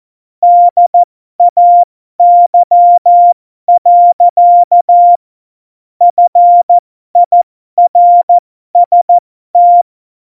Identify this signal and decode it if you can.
Morse code, heard as a keyed tone, about 14 wpm, spaced as for 12 wpm, 700 Hz. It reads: DAY. FIRST
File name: Morsekurs / Witze